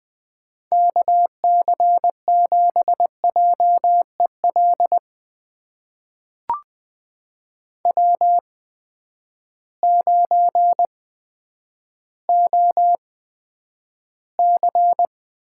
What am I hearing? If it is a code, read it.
Morse code, heard as a keyed tone, about 20 wpm, spaced as for 10 wpm, 700 Hz. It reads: KC7JEL W9OC
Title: USA Call Signs - 8x Character Spacing - 20wpm